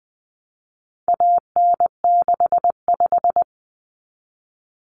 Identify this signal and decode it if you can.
Morse code, heard as a keyed tone, about 20 wpm, 700 Hz. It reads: AN65
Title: Common Cut Numbers - 4-Digit 20wpm